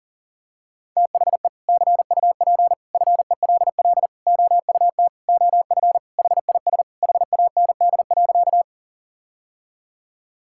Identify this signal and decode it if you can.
Morse code, heard as a keyed tone, about 40 wpm, 700 Hz. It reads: THE CUP FELL OUT OF HIS HAND.
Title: Sentences from Top 1600 Words 40wpm